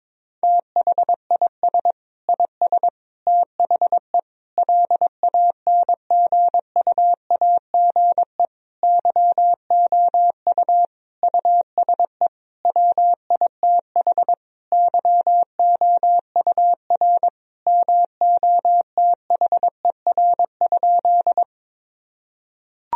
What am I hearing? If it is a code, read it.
Morse code, heard as a keyed tone, about 22 wpm, 700 Hz. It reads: THIS IS THE LANGUAGE YOU USE WITH YOUR MOTHER?
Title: Sentences from Top 1000 Words - Review Part 3 - 22wpm